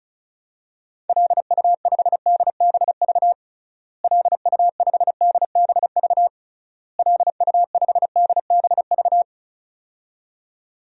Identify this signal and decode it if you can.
Morse code, heard as a keyed tone, about 34 wpm, 700 Hz. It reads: LU5DBV LU5DBV LU5DBV
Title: DXCC-100 Call Signs - 6 Characters - Repeated 3x - 35wpm